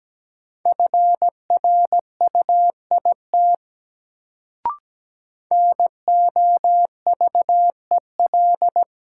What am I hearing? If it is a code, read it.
Morse code, heard as a keyed tone, about 17 wpm, 700 Hz. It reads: FRUIT NOVEL
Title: Sentences from Top 1300 Words - New Word Drill Alt - 17wpm